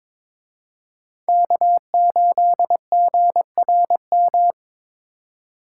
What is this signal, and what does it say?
Morse code, heard as a keyed tone, about 26 wpm, 700 Hz. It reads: K8GRM
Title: Speed Racing - USA 1x3 Call Signs 22wpm